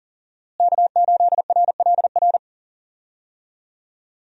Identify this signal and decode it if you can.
Morse code, heard as a keyed tone, about 40 wpm, 700 Hz. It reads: K8RLR
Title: USA Call Signs - 5 Characters - Repeated 3x - 40wpm